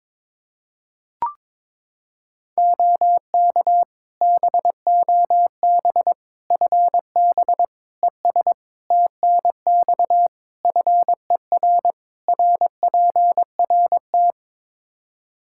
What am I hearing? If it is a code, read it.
Morse code, heard as a keyed tone, about 22 wpm, 700 Hz. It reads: OK BOB FB ES TNX FER RPRT
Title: Practice QSO - Exchange 2 - Info - 22wpm